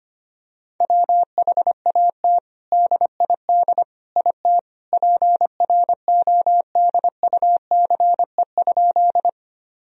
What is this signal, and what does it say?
Morse code, heard as a keyed tone, about 25 wpm, 700 Hz. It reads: WHAT DID IT PRODUCE?
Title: Sentences from Top 1000 Words - Review Part 3 - 25wpm